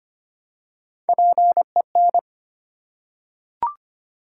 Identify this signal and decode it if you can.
Morse code, heard as a keyed tone, about 25 wpm, 700 Hz. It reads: PEN
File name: practice / 25wpm / Easy Words with Hints - 25wpm